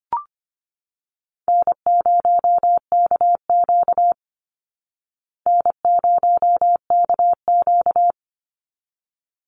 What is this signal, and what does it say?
Morse code, heard as a keyed tone, about 24 wpm, 700 Hz. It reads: N0KQ N0KQ
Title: Easy Call Signs Repeated 2x - 25wpm